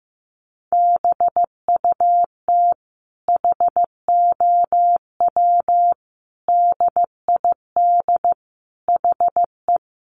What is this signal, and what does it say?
Morse code, heard as a keyed tone, about 15 wpm, 700 Hz. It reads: BUT HOW DID HE
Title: Sentences from Top 900 Words 15wpm